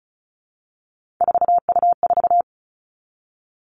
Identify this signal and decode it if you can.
Morse code, heard as a keyed tone, about 36 wpm, 700 Hz. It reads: 4U4
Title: Three Challenging Letter-Number Combinations 35wpm